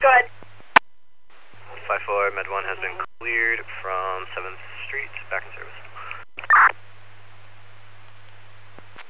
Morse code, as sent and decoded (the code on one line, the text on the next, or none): none